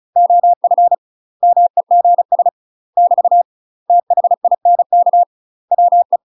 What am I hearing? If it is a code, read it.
Morse code, heard as a keyed tone, about 36 wpm, 700 Hz. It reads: OF MEGS <BT> THINK WE
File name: UK MASTER QSOs TRAINING / 35WPM / UK MASTER QSOs 0011